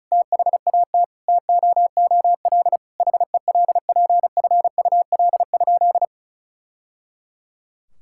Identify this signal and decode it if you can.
Morse code, heard as a keyed tone, about 34 wpm, 700 Hz. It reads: THAT TOOL HELPFUL?
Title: Sentences from Top 1000 Words - Review Part 6 - 35wpm